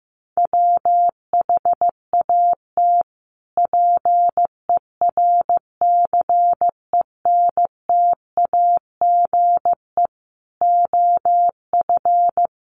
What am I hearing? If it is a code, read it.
Morse code, heard as a keyed tone, about 15 wpm, 700 Hz. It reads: WHAT PERCENTAGE OF
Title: Sentences from Top 1300 Words 15wpm